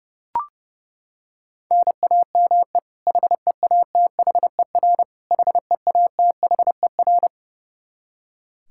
Morse code, heard as a keyed tone, about 30 wpm, 700 Hz. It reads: NAME HEATHER HEATHER
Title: K1USN SST QSO Practice 30wpm